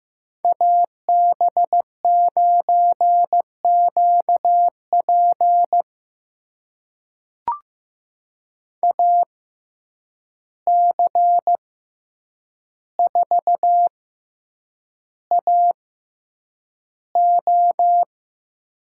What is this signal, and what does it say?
Morse code, heard as a keyed tone, about 15 wpm, spaced as for 8 wpm, 700 Hz. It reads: AB9QP AC4AO
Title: USA Call Signs - 5 Characters - 6x Character Spacing - 15wpm